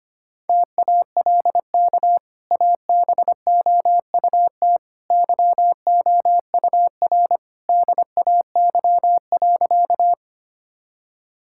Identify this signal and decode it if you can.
Morse code, heard as a keyed tone, about 25 wpm, 690 Hz. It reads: TALK ABOUT YOUR DAY.
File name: Easy Sentences with Hints - 25wpm